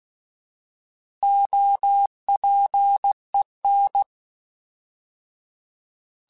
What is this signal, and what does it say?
Morse code, heard as a keyed tone, about 16 wpm, 800 Hz. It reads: OPEN
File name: Alphabetical / Alphabetical mp3s / Os @15